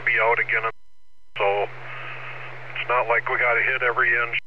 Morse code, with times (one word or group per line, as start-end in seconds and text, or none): none